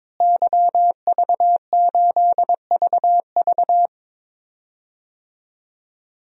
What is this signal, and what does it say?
Morse code, heard as a keyed tone, about 22 wpm, 700 Hz. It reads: YV8VV
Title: DXCC-100 Call Signs - 5 Characters - Repeated 2x - 22wpm